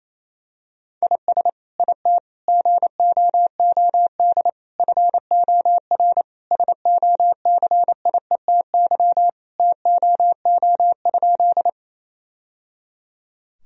Morse code, heard as a keyed tone, about 28 wpm, 700 Hz. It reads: IS IT GOOD FOR SOCIETY TOO?
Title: Sentences from Top 1000 Words - Review Part 4 - 28wpm